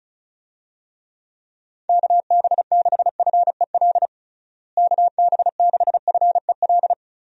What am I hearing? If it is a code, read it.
Morse code, heard as a keyed tone, about 34 wpm, 700 Hz. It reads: KB6FEL KB6FEL
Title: USA Call Signs - 6 Characters - Repeated 2x - 35wpm